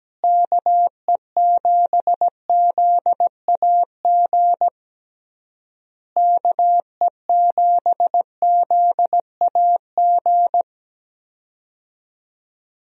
Morse code, heard as a keyed tone, about 17 wpm, 700 Hz. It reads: KE7ZAG KE7ZAG
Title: USA Call Signs - 6 Characters - Repeated 3x - 17wpm